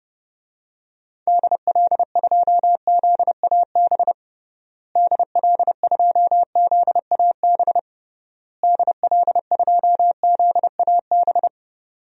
Morse code, heard as a keyed tone, about 30 wpm, 700 Hz. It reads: DL2ZAB DL2ZAB DL2ZAB
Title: DXCC-100 Call Signs - 6 Characters - Repeated 3x - 30wpm